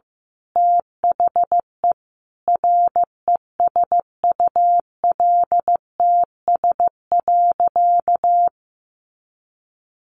Morse code, heard as a keyed tone, about 15 wpm, 700 Hz. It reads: THE RESULTS.
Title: A Sound Of Thunder 15wpm